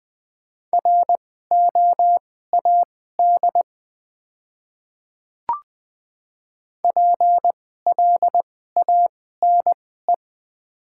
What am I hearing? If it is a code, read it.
Morse code, heard as a keyed tone, about 20 wpm, spaced as for 15 wpm, 690 Hz. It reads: ROAD PLANE
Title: 2x Character Spacing - Top 500 Words - 20wpm